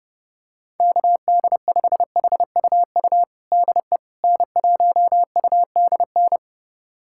Morse code, heard as a keyed tone, about 32 wpm, 700 Hz. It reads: KD5HUU DE N1UDN
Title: USA Callsign DE Callsign 30wpm